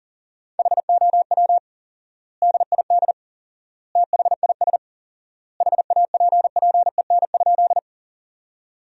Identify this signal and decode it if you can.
Morse code, heard as a keyed tone, about 40 wpm, 700 Hz. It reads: HOW DID THIS HAPPEN?
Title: 4x Word-Spacing - Sentences from Top 400 Words - 40wpm